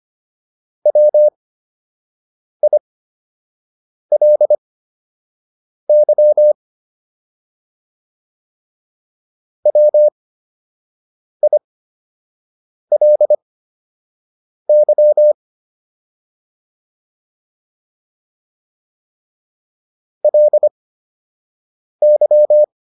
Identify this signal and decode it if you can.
Morse code, heard as a keyed tone, about 25 wpm, spaced as for 6 wpm, 600 Hz. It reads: WILY WILY LY